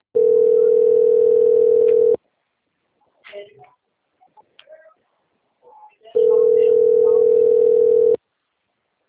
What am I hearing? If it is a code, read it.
Ringback tone (North America), 440 + 480 Hz.
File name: out- - - - - .84958